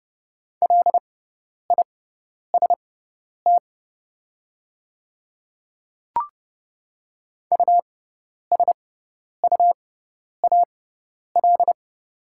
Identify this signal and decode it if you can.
Morse code, heard as a keyed tone, about 30 wpm, spaced as for 10 wpm, 700 Hz. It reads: LIST USUAL
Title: 6x Character Spacing - Top 500 Words - 30wpm